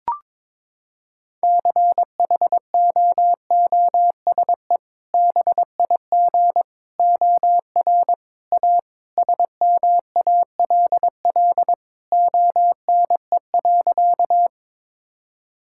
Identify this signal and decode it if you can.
Morse code, heard as a keyed tone, about 22 wpm, 700 Hz. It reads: CHOOSE BIG OR A SMALL ONE.
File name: practice / 22wpm / Sentences with 2-Word Phrases - 22wpm